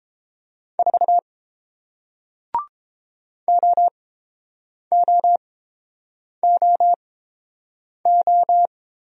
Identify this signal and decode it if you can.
Morse code, heard as a keyed tone, about 30 wpm, spaced as for 7 wpm, 700 Hz. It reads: 4 OOOO